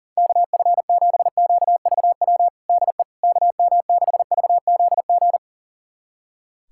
Morse code, heard as a keyed tone, about 40 wpm, 700 Hz. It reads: KF7QVW DE KM6VZG
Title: USA Callsign DE Callsign 40wpm